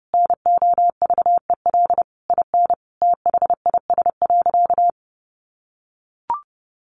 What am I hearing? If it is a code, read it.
Morse code, heard as a keyed tone, about 30 wpm, 700 Hz. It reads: NOVEL IN THIS.
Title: Sentences from Top 1300 Words 30wpm